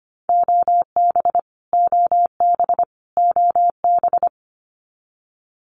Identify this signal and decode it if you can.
Morse code, heard as a keyed tone, about 25 wpm, 700 Hz. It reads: OB OB OB
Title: Abbreviations Repeated 3x - 25wpm